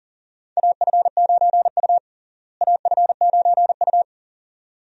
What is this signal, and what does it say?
Morse code, heard as a keyed tone, about 37 wpm, 700 Hz. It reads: AF9U AF9U
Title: USA Call Signs - 4 Characters - Repeated 2x - 40wpm